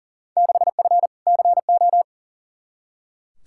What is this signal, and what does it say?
Morse code, heard as a keyed tone, about 40 wpm, 700 Hz. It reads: 6F CO
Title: Field Day Exchanges Encore - 40wpm